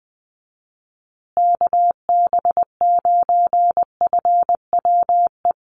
Morse code, heard as a keyed tone, about 20 wpm, 700 Hz. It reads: KB9FWE
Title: USA Call Signs Repeated 2x 20wpm